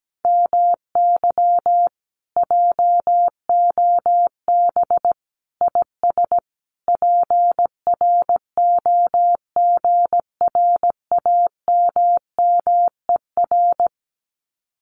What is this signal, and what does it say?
Morse code, heard as a keyed tone, about 17 wpm, 700 Hz. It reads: MY JOB IS PROGRAMMER
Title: Common QSO Phrases Encore 17wpm